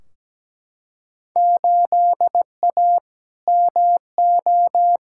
Morse code, heard as a keyed tone, about 17 wpm, 700 Hz. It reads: 8A MO
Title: Field Day Exchanges Encore - 17wpm